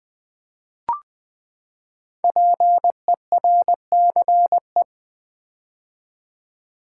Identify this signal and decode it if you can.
Morse code, heard as a keyed tone, about 20 wpm, 700 Hz. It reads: PERCE
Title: Build A Word - 5 Letters At A Time - 10-Letter Words - 20wpm